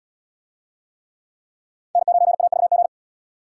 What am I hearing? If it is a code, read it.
Morse code, heard as a keyed tone, about 77 wpm, 700 Hz. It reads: N9AVG